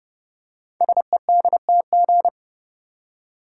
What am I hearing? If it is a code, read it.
Morse code, heard as a keyed tone, about 30 wpm, 700 Hz. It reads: SEDTG
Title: Code Groups - 5 Letter:Numbers - 30wpm